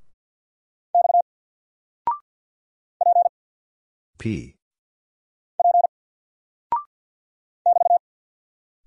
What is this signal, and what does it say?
Morse code, heard as a keyed tone, about 48 wpm, 700 Hz. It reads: X P P <BT>
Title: Single Character (Letter:Number:Punctuation:Prosign) Alt 50wpm